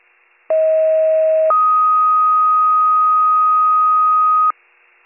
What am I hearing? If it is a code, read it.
Two-tone page, heard as a one-second tone, then a three-second tone, about 640 then 1250 Hz. It is heard from 0.5 to 4.5 s.